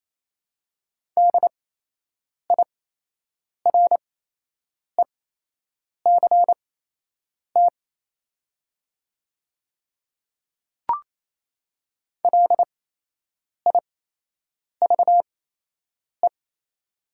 Morse code, heard as a keyed tone, about 27 wpm, spaced as for 8 wpm, 690 Hz. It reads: DIRECT LIVE
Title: 8x Character Spacing - Top 500 Words - 28wpm